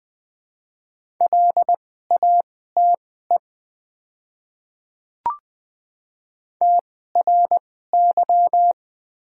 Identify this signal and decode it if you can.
Morse code, heard as a keyed tone, about 20 wpm, spaced as for 15 wpm, 700 Hz. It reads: LATE TRY
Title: 2x Character Spacing - Top 300 Words - 20wpm